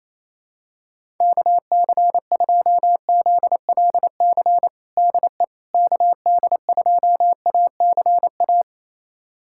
Morse code, heard as a keyed tone, about 28 wpm, 700 Hz. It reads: KC2ZLC DE KD2ACA
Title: USA Callsign DE Callsign 28wpm